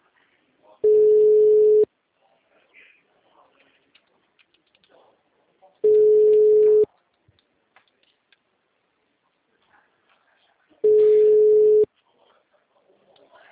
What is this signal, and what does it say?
Ringback tone (Europe, 425 Hz), with speech.